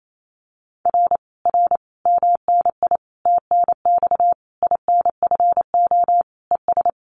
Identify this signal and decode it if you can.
Morse code, heard as a keyed tone, about 28 wpm, 700 Hz. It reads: R R MNI TNX INFO ES
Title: Common QSO Phrases Encore 28wpm